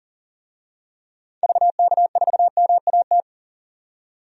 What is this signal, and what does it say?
Morse code, heard as a keyed tone, about 40 wpm, 700 Hz. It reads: VK4MAT
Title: DXCC-100 Call Signs - 6 Characters - 40wpm